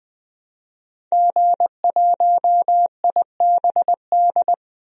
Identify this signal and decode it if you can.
Morse code, heard as a keyed tone, about 20 wpm, 700 Hz. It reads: G1IBD
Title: DXCC-100 Call Signs - 5 Characters - 20wpm